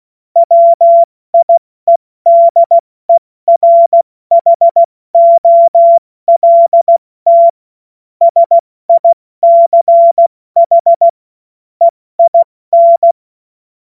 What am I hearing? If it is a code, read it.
Morse code, heard as a keyed tone, about 16 wpm, spaced as for 14 wpm, 680 Hz. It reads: WIEDERHOLT SICH EIN
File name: Morsekurs / Witze / Witz 25. Mai 2021, 70 F80